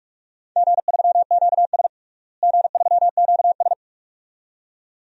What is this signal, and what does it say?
Morse code, heard as a keyed tone, about 46 wpm, 700 Hz. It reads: G3QS G3QS
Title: DXCC-100 Call Signs - 4 Characters - Repeated 2x - 45wpm